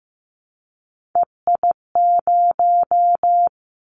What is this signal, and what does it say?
Morse code, heard as a keyed tone, about 15 wpm, 700 Hz. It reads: EI0
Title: Callsign Prefixes for Top 50 DXCC - 15wpm